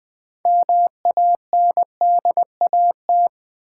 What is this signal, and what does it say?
Morse code, heard as a keyed tone, about 20 wpm, 700 Hz. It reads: MANDAT
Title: Build A Word - 3 Letters At A Time - 9-Letter Words - 20wpm